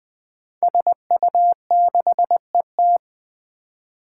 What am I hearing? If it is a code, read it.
Morse code, heard as a keyed tone, about 21 wpm, 700 Hz. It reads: SU6ET
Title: Five Challenging Letter-Number Combinations 20wpm